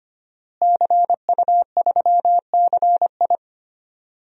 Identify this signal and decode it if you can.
Morse code, heard as a keyed tone, about 25 wpm, 700 Hz. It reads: CU3CI
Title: DXCC-100 Call Signs - 5 Characters - Repeated 2x - 25wpm